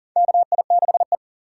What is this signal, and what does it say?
Morse code, heard as a keyed tone, about 40 wpm, 700 Hz. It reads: KI6E
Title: USA Call Signs - 4 Characters - 4x Character Spacing - 40wpm